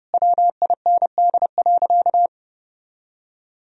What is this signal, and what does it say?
Morse code, heard as a keyed tone, about 32 wpm, 700 Hz. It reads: WIND.
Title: Sentences from Top 1000 Words - Review Part 4 - 30wpm